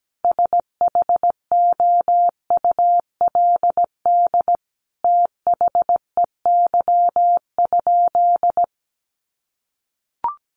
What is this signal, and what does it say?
Morse code, heard as a keyed tone, about 17 wpm, 700 Hz. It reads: SHOULD THEY?
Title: Sentences from Top 300 Words 17wpm